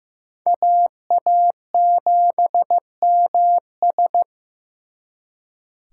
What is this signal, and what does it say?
Morse code, heard as a keyed tone, about 18 wpm, 700 Hz. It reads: AA7MS
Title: Speed Racing - USA 2x2 Call Signs 15wpm